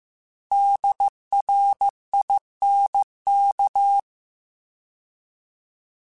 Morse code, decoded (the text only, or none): DRINK